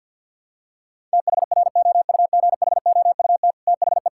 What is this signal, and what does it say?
Morse code, heard as a keyed tone, about 50 wpm, 700 Hz. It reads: THROUGHOUT THE
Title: Sentences from Top 1000 Words - Review Part 5 - 50wpm